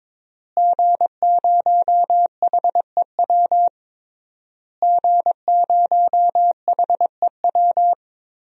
Morse code, heard as a keyed tone, about 22 wpm, 700 Hz. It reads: G0HEW G0HEW
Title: DXCC-100 Call Signs - 5 Characters - Repeated 3x - 22wpm